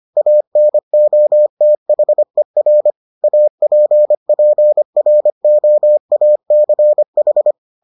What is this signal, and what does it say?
Morse code, heard as a keyed tone, about 25 wpm, 600 Hz. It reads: ANOTHER APPROACH